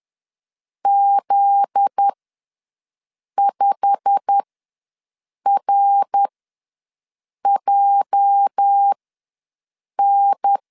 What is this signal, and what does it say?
Morse code, heard as a keyed tone, about 10 wpm, spaced as for 6 wpm, 800 Hz. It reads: Z5RJN
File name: LEZIONI / 8 YJ45 / 8 YJ45 ws 10wpm 3843